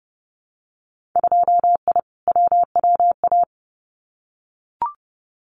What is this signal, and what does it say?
Morse code, heard as a keyed tone, about 30 wpm, 700 Hz. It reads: 2I WWA
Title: Winter Field Day Exchanges Encore 30wpm